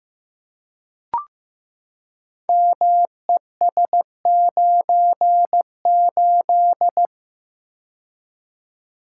Morse code, heard as a keyed tone, about 15 wpm, 700 Hz. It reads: MES98
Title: Code Groups - 5 Letter:Numbers - 15wpm